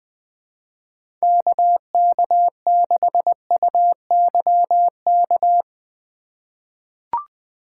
Morse code, heard as a keyed tone, about 20 wpm, 700 Hz. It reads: KK6UYK
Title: USA Call Signs - 6 Characters - 4x Character Spacing - 20wpm